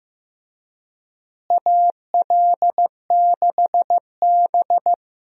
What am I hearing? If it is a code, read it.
Morse code, heard as a keyed tone, about 15 wpm, 700 Hz. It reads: AL6B
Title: USA Call Signs - 4 Characters - Repeated 3x - 15wpm